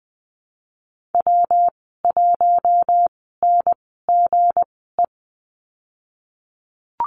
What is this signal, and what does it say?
Morse code, heard as a keyed tone, about 20 wpm, spaced as for 15 wpm, 700 Hz. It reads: W1NGE
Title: 186 Lesson 30g - Letter b - Call Signs 20wpm